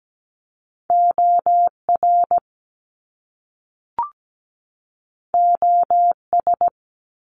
Morse code, heard as a keyed tone, about 17 wpm, 700 Hz. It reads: OR OS
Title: Top Two Letter Combinations in Words 17wpm